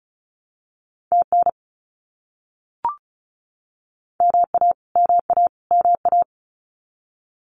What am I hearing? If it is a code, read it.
Morse code, heard as a keyed tone, about 34 wpm, 700 Hz. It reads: TN MA MA MA
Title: US States Repeated 3x 35wpm